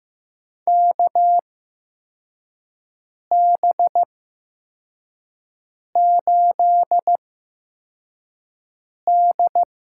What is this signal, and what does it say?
Morse code, heard as a keyed tone, about 15 wpm, spaced as for 4 wpm, 700 Hz. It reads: KB8D
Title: USA Call Signs - 8x Character Spacing - 15wpm